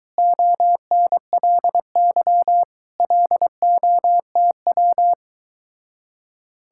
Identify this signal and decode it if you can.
Morse code, heard as a keyed tone, about 22 wpm, 700 Hz. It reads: ONLY LOTW